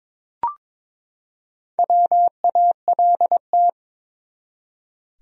Morse code, heard as a keyed tone, about 22 wpm, 700 Hz. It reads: WALT
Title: USA Common Names - Max 5 Letters - 22wpm